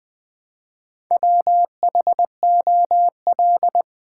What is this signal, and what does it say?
Morse code, heard as a keyed tone, about 20 wpm, 700 Hz. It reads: WHOL
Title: Build A Word - 1 Letter At A Time - 5-Letter Words - 20wpm